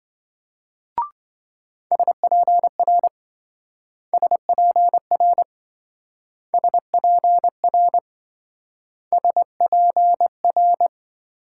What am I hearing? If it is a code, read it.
Morse code, heard as a keyed tone, about 27 wpm, 700 Hz. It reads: SPR SPR SPR SPR